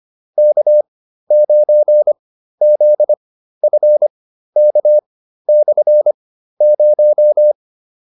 Morse code, heard as a keyed tone, about 25 wpm, spaced as for 13 wpm, 600 Hz. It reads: K9ZFK/0